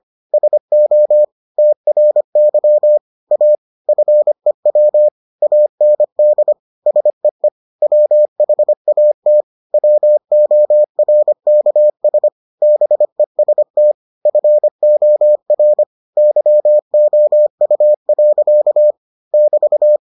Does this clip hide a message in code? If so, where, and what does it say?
Yes: Morse code from 0.3 to 20.0 s, heard as a keyed tone, about 25 wpm, 600 Hz. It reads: SO TRY A FEW AND SEE WHAT WORKS BEST FOR YOU. <BT>